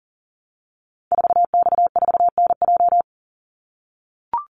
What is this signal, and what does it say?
Morse code, heard as a keyed tone, about 40 wpm, 700 Hz. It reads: 4X4NJ
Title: Call Signs - 40wpm